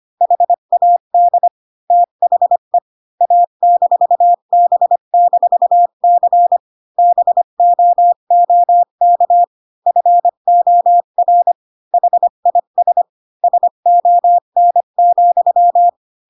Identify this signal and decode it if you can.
Morse code, heard as a keyed tone, about 25 wpm, 700 Hz. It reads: HAD THE A-B-C BOOK FOR HIS SON,
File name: Pinocchio / Pinocchio ch8 20 25 ik1hge